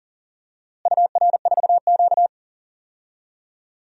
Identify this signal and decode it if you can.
Morse code, heard as a keyed tone, about 40 wpm, 700 Hz. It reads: UR4Q
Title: DXCC-100 Call Signs - 4 Characters - Repeated 2x - 40wpm